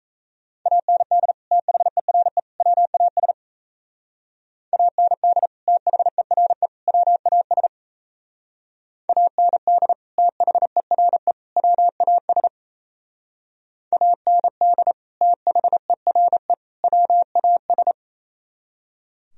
Morse code, heard as a keyed tone, about 35 wpm, 700 Hz. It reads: AND THERE WAS AND THERE WAS AND THERE WAS AND THERE WAS